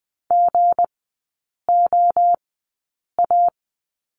Morse code, heard as a keyed tone, about 20 wpm, spaced as for 8 wpm, 690 Hz. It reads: GOA
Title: 205 Lesson 33e - Letter j - Three Characters 20wpm